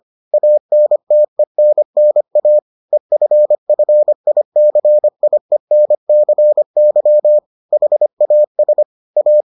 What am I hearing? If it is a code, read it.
Morse code, heard as a keyed tone, about 25 wpm, 600 Hz. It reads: ANTENNA EFFICIENCY HAS A